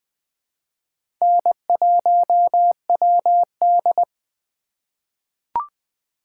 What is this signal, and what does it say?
Morse code, heard as a keyed tone, about 20 wpm, 700 Hz. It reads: N1WD
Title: USA Call Signs - 4x Character Spacing - 20wpm